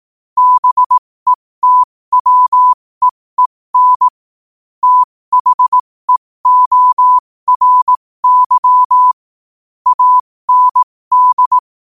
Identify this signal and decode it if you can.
Morse code, heard as a keyed tone, about 18 wpm, spaced as for 15 wpm, 1000 Hz. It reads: BETWEEN THEORY AND
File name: qotdfiles / 15wpm